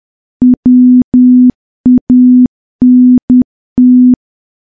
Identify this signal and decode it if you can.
Morse code, heard as a keyed tone, about 10 wpm, 250 Hz. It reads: WANT
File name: morse / morse-training-10WPM